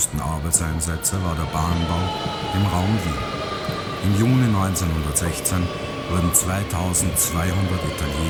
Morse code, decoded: none

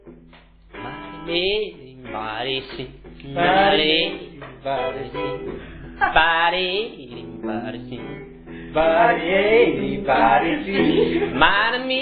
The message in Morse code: none

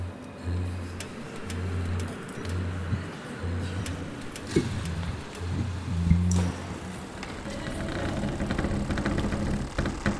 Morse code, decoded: none

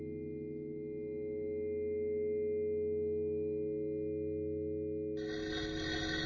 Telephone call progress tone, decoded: dial tone

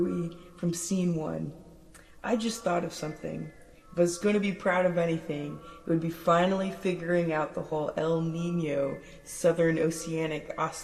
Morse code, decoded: none

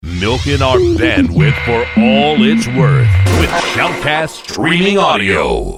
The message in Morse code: none

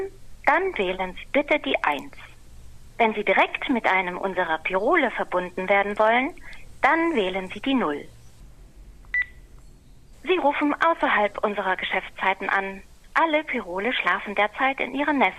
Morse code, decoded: none

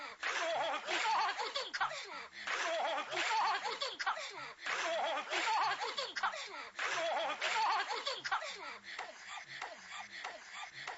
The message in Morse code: none

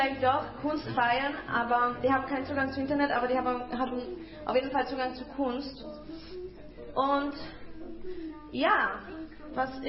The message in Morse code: none

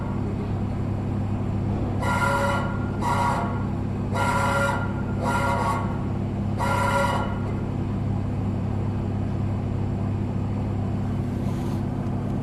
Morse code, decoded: none

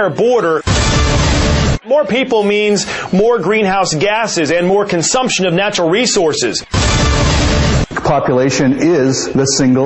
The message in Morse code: none

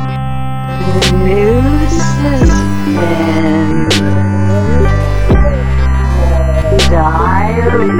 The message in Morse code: none

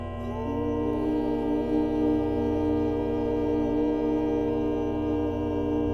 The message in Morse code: none

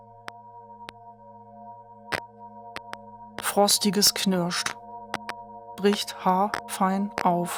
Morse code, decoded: none

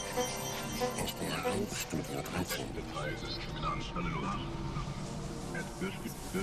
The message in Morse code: none

